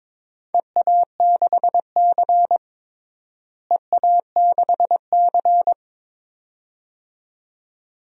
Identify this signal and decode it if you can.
Morse code, heard as a keyed tone, about 22 wpm, 700 Hz. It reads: EA6C EA6C
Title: DXCC-100 Call Signs - 4 Characters - Repeated 2x - 22wpm